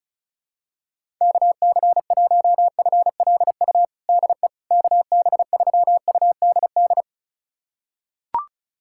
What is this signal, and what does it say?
Morse code, heard as a keyed tone, about 36 wpm, 700 Hz. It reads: KC1FLU DE KB3UDD
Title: USA Callsign DE Callsign - 6 Characters - 35wpm